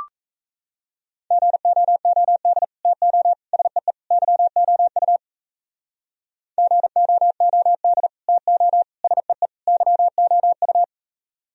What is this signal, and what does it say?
Morse code, heard as a keyed tone, about 37 wpm, 700 Hz. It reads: GOOD TO SEE YOU GOOD TO SEE YOU